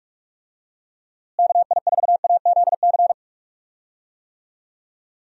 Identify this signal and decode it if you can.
Morse code, heard as a keyed tone, about 46 wpm, 700 Hz. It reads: KI4AZC
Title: USA Call Signs - 6 Characters - Repeated 3x - 45wpm